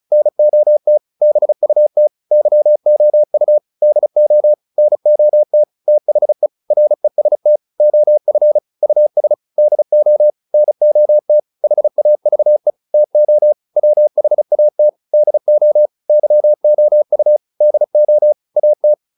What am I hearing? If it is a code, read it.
Morse code, heard as a keyed tone, about 34 wpm, 600 Hz. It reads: NOT BUT YOU DO NOT THE REST OF US DO NOT HAVE TO WHAT DO YOU DO AT